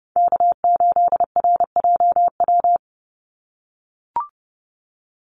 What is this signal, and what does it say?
Morse code, heard as a keyed tone, about 30 wpm, 700 Hz. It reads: K8RJW
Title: Call Signs - 30wpm